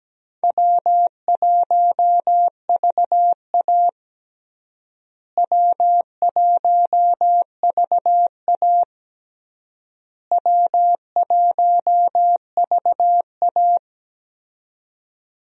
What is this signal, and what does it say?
Morse code, heard as a keyed tone, about 17 wpm, 700 Hz. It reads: W1VA W1VA W1VA
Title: USA Call Signs - 4 Characters - Repeated 3x - 17wpm